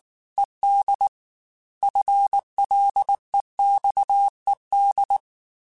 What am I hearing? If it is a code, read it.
Morse code, heard as a keyed tone, about 19 wpm, 800 Hz. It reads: ED FLEXED
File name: FL / FL mp3s / Fast FL @18